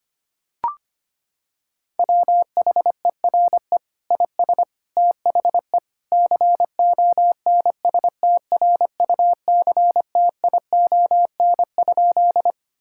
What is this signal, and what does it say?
Morse code, heard as a keyed tone, about 25 wpm, 700 Hz. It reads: WHERE IS THE CONSTRUCTION?
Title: Sentences from Top 1600 Words 25wpm